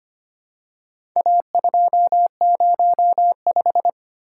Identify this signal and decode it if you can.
Morse code, heard as a keyed tone, about 25 wpm, 700 Hz. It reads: A205
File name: practice / 25wpm / Four Letter-Number Combinations - 25wpm